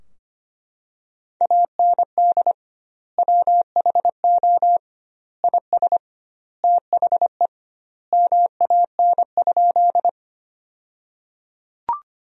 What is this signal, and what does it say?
Morse code, heard as a keyed tone, about 25 wpm, 700 Hz. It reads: AND WHO IS THE MAN?
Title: 2x Word-Spacing - Sentences from Top 200 Words - 25wpm